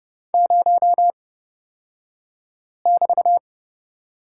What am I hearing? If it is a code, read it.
Morse code, heard as a keyed tone, about 30 wpm, spaced as for 5 wpm, 700 Hz. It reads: 0<BT>
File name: Lesson 37d - Letter x - Two Characters 30wpm